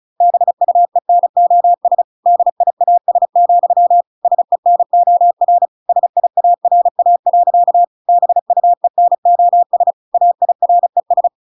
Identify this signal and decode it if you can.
Morse code, heard as a keyed tone, about 35 wpm, 700 Hz. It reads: BUENOS DIAS, SENOR SIARA. BUENOS AIRES